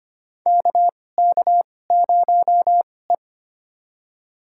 Morse code, heard as a keyed tone, about 25 wpm, spaced as for 18 wpm, 700 Hz. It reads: KK0E
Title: USA Call Signs - 4 Characters - 2x Character Spacing - 25wpm